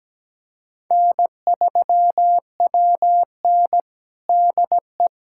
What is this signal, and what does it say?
Morse code, heard as a keyed tone, about 17 wpm, 700 Hz. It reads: N3WN DE
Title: USA Callsign DE Callsign - 4 Characters - 17wpm